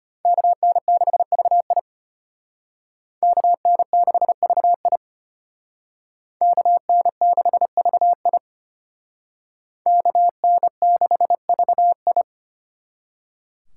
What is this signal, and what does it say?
Morse code, heard as a keyed tone, about 32 wpm, 700 Hz. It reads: KN6VI KN6VI KN6VI KN6VI